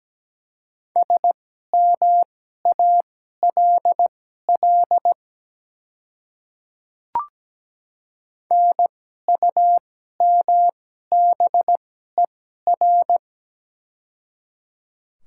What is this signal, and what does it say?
Morse code, heard as a keyed tone, about 17 wpm, spaced as for 12 wpm, 700 Hz. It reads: SMALL NUMBER
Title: 2x Character Spacing - Top 200 Words - 17wpm